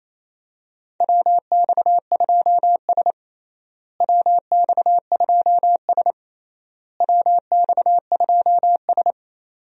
Morse code, heard as a keyed tone, about 28 wpm, 700 Hz. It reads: WX2S WX2S WX2S
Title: USA Call Signs - 4 Characters - Repeated 3x - 28wpm